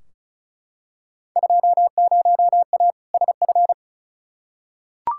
Morse code, heard as a keyed tone, about 35 wpm, 700 Hz. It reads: 20A SF
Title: Field Day Exchanges Encore - 35wpm